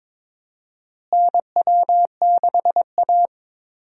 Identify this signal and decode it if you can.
Morse code, heard as a keyed tone, about 22 wpm, 700 Hz. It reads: NW6A
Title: USA Call Signs - 4 Characters - Repeated 2x - 22wpm